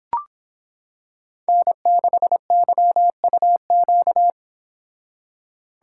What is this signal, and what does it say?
Morse code, heard as a keyed tone, about 26 wpm, 700 Hz. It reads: N6YUQ